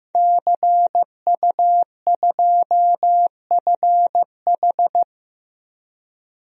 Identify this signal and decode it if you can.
Morse code, heard as a keyed tone, about 16 wpm, 700 Hz. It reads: CU2FH
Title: DXCC-100 Call Signs - 5 Characters - Repeated 3x - 15wpm